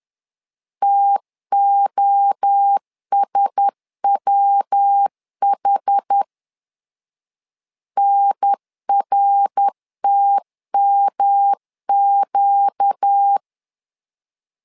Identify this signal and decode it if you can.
Morse code, heard as a keyed tone, about 10 wpm, 800 Hz. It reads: TOSWH NRTMQ
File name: LEZIONI / 4 Q1S 10wpm 2133